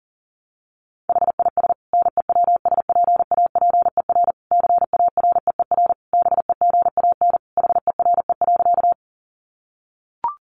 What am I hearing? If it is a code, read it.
Morse code, heard as a keyed tone, about 40 wpm, 700 Hz. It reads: HIS NEWSPAPER CAREER BEGAN HERE.
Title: Sentences from Top 1100 Words 40wpm